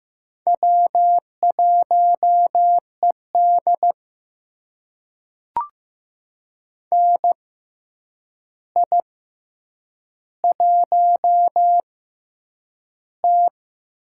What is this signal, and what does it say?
Morse code, heard as a keyed tone, about 15 wpm, spaced as for 8 wpm, 700 Hz. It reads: W1ED NI1T
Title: USA Call Signs - 4 Characters - 6x Character Spacing - 15wpm